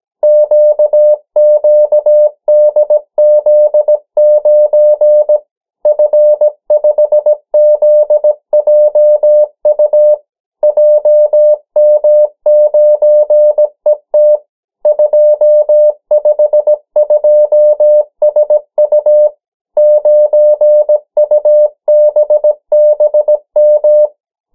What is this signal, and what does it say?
Morse code, heard as a keyed tone, about 17 wpm, 600 Hz. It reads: QQDZ9 F5ZJU JM9ET 252SU 9UBBM
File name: morse mp3 / 14 urok 13